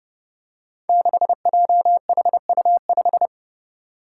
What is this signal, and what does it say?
Morse code, heard as a keyed tone, about 30 wpm, 700 Hz. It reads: 6JHU5
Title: Five Challenging Letter-Number Combinations 30wpm